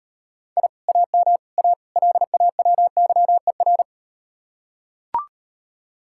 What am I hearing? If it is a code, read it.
Morse code, heard as a keyed tone, about 38 wpm, 700 Hz. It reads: I AM A LAWYER